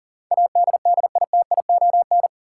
Morse code, heard as a keyed tone, about 40 wpm, 700 Hz. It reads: ADDITION
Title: Sentences from Top 1000 Words - Review Part 6 - New Word Drill - 40wpm